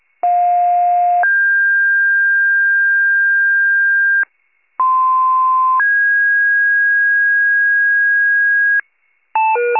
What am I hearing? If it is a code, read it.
Two-tone page, heard as a one-second tone, then a three-second tone, about 710 then 1600 Hz, about 1040 then 1650 Hz. It is heard from 0.2 to 8.8 s.
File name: 95 RESCUE DUTY